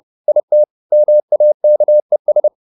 Morse code, heard as a keyed tone, about 30 wpm, 600 Hz. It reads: IT MAKES